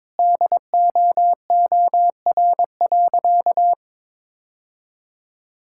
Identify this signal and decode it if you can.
Morse code, heard as a keyed tone, about 21 wpm, 690 Hz. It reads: DOOR.
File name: practice / 22wpm / Easy Sentences with Hints - 22wpm